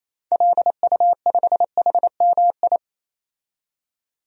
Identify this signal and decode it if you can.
Morse code, heard as a keyed tone, about 28 wpm, 700 Hz. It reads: LU5HMI
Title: DXCC-100 Call Signs - 6 Characters - 28wpm